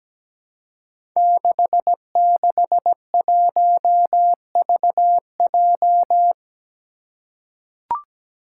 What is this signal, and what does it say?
Morse code, heard as a keyed tone, about 17 wpm, 700 Hz. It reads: 661VJ
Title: Five Challenging Letter-Number Combinations 17wpm